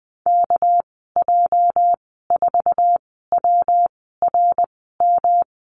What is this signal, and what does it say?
Morse code, heard as a keyed tone, about 19 wpm, spaced as for 14 wpm, 700 Hz. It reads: KJ4WRM
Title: 207 Lesson 33g - Letter j - Call Signs 20wpm